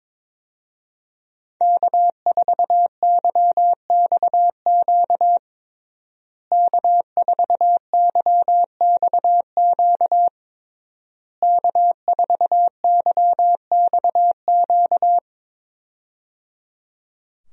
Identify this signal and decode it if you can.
Morse code, heard as a keyed tone, about 22 wpm, 700 Hz. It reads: K4YXQ K4YXQ K4YXQ
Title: USA Call Signs - 5 Characters - Repeated 3x - 22wpm